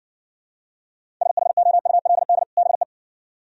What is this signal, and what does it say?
Morse code, heard as a keyed tone, about 67 wpm, 700 Hz. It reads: SHOULD BE